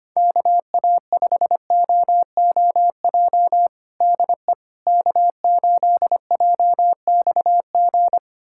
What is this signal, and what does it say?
Morse code, heard as a keyed tone, about 25 wpm, 700 Hz. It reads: KA5OOJ DE K8JXG
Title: USA Callsign DE Callsign 25wpm